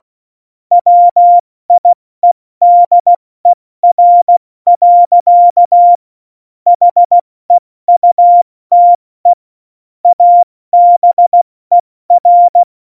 Morse code, heard as a keyed tone, about 16 wpm, 700 Hz. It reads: WIEDER. HEUTE ABER
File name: Morsekurs / Witze / Witz 16. April 2019, 70 F80